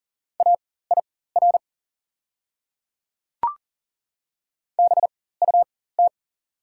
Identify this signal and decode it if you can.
Morse code, heard as a keyed tone, about 40 wpm, spaced as for 19 wpm, 700 Hz. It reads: AIR BUT
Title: 4x Character Spacing - Top 200 Words - 40wpm